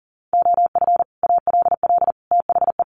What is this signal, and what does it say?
Morse code, heard as a keyed tone, about 40 wpm, 700 Hz. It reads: OF ALL THE
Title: Sets of 3 Words 40wpm